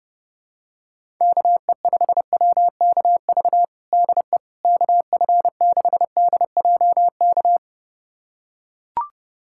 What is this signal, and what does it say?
Morse code, heard as a keyed tone, about 32 wpm, 700 Hz. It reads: KE5WKV DE KF6DJK
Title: USA Callsign DE Callsign - 6 Characters - 30wpm